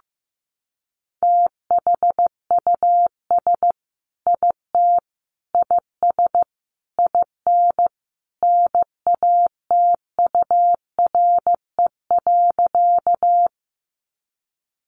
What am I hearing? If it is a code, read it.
Morse code, heard as a keyed tone, about 14 wpm, 700 Hz. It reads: THUS IT IS IN NATURE.
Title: Sentences from Top 800 Words 15wpm